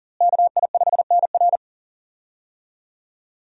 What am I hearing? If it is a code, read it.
Morse code, heard as a keyed tone, about 40 wpm, 700 Hz. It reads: KI5NR
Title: USA Call Signs - 5 Characters - Repeated 2x - 40wpm